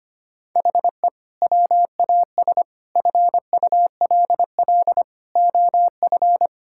Morse code, heard as a keyed tone, about 25 wpm, 700 Hz. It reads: HE WAS FULL OF
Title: Sentences from Top 1000 Words - Review Part 6 - 25wpm